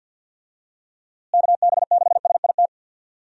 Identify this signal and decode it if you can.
Morse code, heard as a keyed tone, about 51 wpm, 700 Hz. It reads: KB6SIT
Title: USA Call Signs - 6 Characters - Repeated 3x - 50wpm